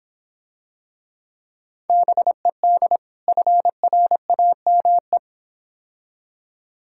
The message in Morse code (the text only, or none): BED FRAME